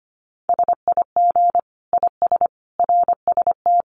Morse code, heard as a keyed tone, about 25 wpm, 700 Hz. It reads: SIG IS RST
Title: Common QSO Phrases 25wpm